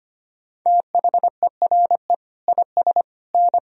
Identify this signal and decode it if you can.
Morse code, heard as a keyed tone, about 25 wpm, 700 Hz. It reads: THERE IS N
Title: Sentences from Top 500 Words and Abbreviations 25wpm